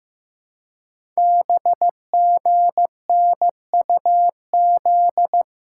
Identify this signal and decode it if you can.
Morse code, heard as a keyed tone, about 15 wpm, 700 Hz. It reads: BGNUZ
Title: Code Groups - 5 Letters - 15wpm